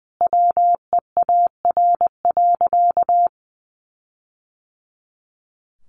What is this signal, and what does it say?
Morse code, heard as a keyed tone, about 20 wpm, 700 Hz. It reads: WEAR.
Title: Sentences from Top 1500 Words 20wpm